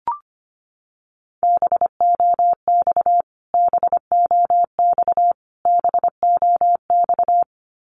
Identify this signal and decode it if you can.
Morse code, heard as a keyed tone, about 25 wpm, 700 Hz. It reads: BOX BOX BOX
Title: Top 500 Words Repeated 25wpm